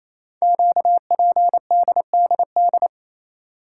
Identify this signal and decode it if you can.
Morse code, heard as a keyed tone, about 28 wpm, 700 Hz. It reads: QPDDD
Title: Code Groups - 5 Letters - 28wpm